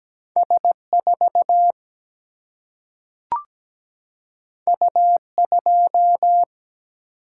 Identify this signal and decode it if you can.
Morse code, heard as a keyed tone, about 17 wpm, 700 Hz. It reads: S4 U2
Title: Single-Letter Single-Number 17wpm